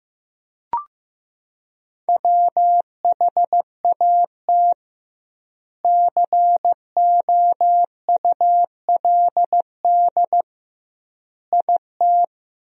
Morse code, heard as a keyed tone, about 15 wpm, 700 Hz. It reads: WHAT COULD IT
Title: 2x Word-Spacing - Sentences from Top 100 Words - 15wpm